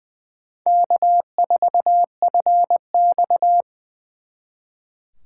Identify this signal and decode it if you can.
Morse code, heard as a keyed tone, about 23 wpm, 700 Hz. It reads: K4FX
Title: Speed Racing - USA 1x2 Call Signs 20wpm